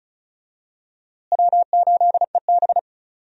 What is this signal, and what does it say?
Morse code, heard as a keyed tone, about 34 wpm, 700 Hz. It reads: W8EB
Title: USA Call Signs - 4 Characters - Repeated 2x - 35wpm